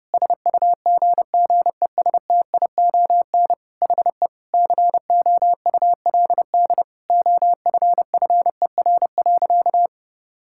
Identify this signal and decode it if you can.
Morse code, heard as a keyed tone, about 30 wpm, 700 Hz. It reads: SUGGESTION HE COULD OFFER.
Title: Sentences from Top 1600 Words 30wpm